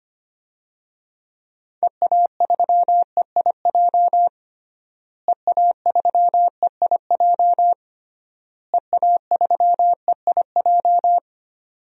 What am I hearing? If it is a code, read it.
Morse code, heard as a keyed tone, about 25 wpm, 700 Hz. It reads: EA3EIJ EA3EIJ EA3EIJ
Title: DXCC-100 Call Signs - 6 Characters - Repeated 3x - 25wpm